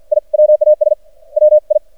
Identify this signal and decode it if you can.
Morse code, heard as a keyed tone, about 53 wpm, 610 Hz. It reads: I WAS WI
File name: archived / CRSnet